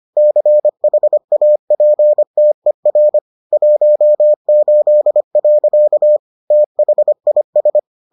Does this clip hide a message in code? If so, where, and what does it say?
Yes: Morse code from 0.2 to 7.8 s, heard as a keyed tone, about 25 wpm, 600 Hz. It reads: CHAPTER 18. THIS